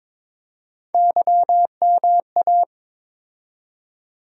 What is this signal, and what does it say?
Morse code, heard as a keyed tone, about 21 wpm, 700 Hz. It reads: YMA
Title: USA Call Sign Postfixes - 3 Letters 22wpm